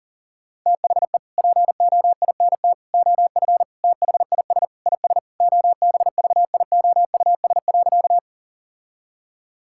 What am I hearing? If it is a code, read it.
Morse code, heard as a keyed tone, about 40 wpm, 700 Hz. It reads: THE POINT OF THIS IS OBVIOUS.
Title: Sentences from Top 1600 Words 40wpm